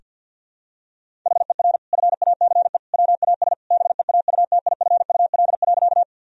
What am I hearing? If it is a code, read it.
Morse code, heard as a keyed tone, about 50 wpm, 700 Hz. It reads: HER FACE WAS BEAUTIFUL.
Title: Sentences from Top 1000 Words - Review Part 2 - 50wpm